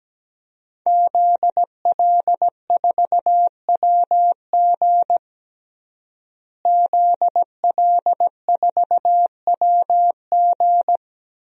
Morse code, heard as a keyed tone, about 17 wpm, 700 Hz. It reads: ZL4WG ZL4WG
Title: DXCC-100 Call Signs - 5 Characters - Repeated 3x - 17wpm